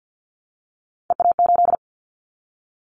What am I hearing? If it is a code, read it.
Morse code, heard as a keyed tone, about 50 wpm, 700 Hz. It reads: EA8
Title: Callsign Prefixes for Top 50 DXCC - 50wpm